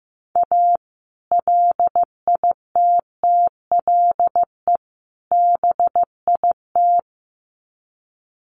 Morse code, heard as a keyed tone, about 15 wpm, 700 Hz. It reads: A LITTLE BIT
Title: Sets of 3 Words 15wpm